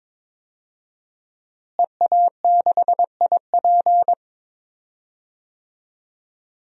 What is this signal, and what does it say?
Morse code, heard as a keyed tone, about 21 wpm, 700 Hz. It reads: EA6IP
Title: DXCC-100 Call Signs - 5 Characters - Repeated 3x - 22wpm